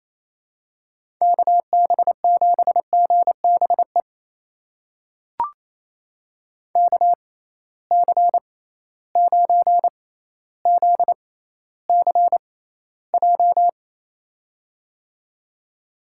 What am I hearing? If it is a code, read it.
Morse code, heard as a keyed tone, about 28 wpm, spaced as for 14 wpm, 700 Hz. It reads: KB7GBE KC9ZCJ
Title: USA Call Signs - 6 Characters - 6x Character Spacing - 28wpm